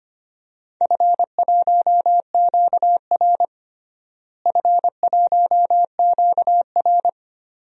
Morse code, heard as a keyed tone, about 25 wpm, 700 Hz. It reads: F1QR F1QR
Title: DXCC-100 Call Signs - 4 Characters - Repeated 2x - 25wpm